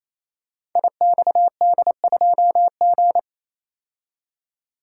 Morse code, heard as a keyed tone, about 29 wpm, 700 Hz. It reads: IXD2G
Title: Code Groups - 5 Letter:Numbers - 28wpm